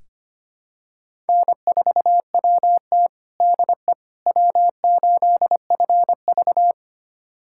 Morse code, heard as a keyed tone, about 25 wpm, 700 Hz. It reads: N4WT DE W8FV
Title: USA Callsign DE Callsign - 4 Characters - 25wpm